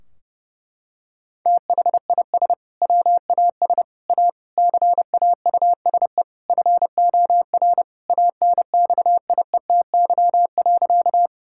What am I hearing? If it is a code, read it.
Morse code, heard as a keyed tone, about 30 wpm, 700 Hz. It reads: THIS WAS A CAUSE FOR ANXIETY.
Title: Sentences from Top 1600 Words 30wpm